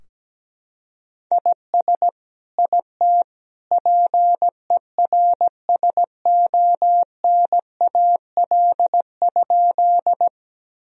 Morse code, heard as a keyed tone, about 17 wpm, 700 Hz. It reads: IS IT PERSONAL?
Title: Sentences from Top 1000 Words - Review Part 4 - 17wpm